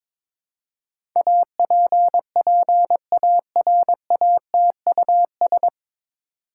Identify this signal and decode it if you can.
Morse code, heard as a keyed tone, about 22 wpm, 700 Hz. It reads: APPARATUS
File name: 9 Letter Words - First 3 Letters Given - 22wpm